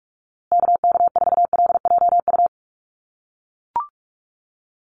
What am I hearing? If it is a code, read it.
Morse code, heard as a keyed tone, about 44 wpm, 700 Hz. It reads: KK4LJU
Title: USA Call Signs Repeated 2x 45wpm